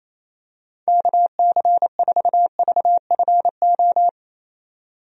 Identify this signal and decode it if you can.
Morse code, heard as a keyed tone, about 28 wpm, 700 Hz. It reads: KC4VFO
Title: USA Call Signs - 6 Characters - 28wpm